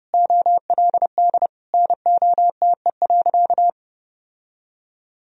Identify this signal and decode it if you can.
Morse code, heard as a keyed tone, about 30 wpm, 700 Hz. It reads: OLD NOTE.
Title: Sentences from Top 1000 Words - Review Part 2 - 30wpm